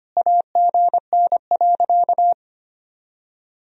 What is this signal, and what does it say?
Morse code, heard as a keyed tone, about 25 wpm, 700 Hz. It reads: AGN.
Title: Sentences from Top 500 Words and Abbreviations 25wpm